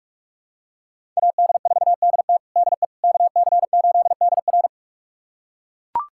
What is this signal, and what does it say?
Morse code, heard as a keyed tone, about 41 wpm, 700 Hz. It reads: AD4DT DE KC8DR
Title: USA Callsign DE Callsign - 5 Characters - 45wpm